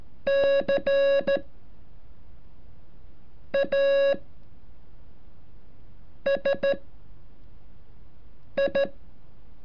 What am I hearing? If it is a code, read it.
Morse code, heard as a keyed tone, about 14 wpm, spaced as for 4 wpm, 550 Hz. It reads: CASI
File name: curso / ejercicio 33